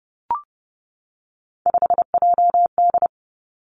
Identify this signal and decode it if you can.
Morse code, heard as a keyed tone, about 30 wpm, 700 Hz. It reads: 5JD
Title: Three Challenging Letter-Number Combinations 30wpm